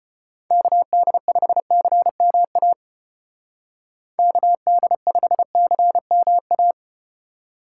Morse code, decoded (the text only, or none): KD5CMA KD5CMA